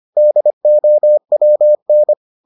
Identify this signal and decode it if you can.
Morse code, heard as a keyed tone, about 25 wpm, 600 Hz. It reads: DOWN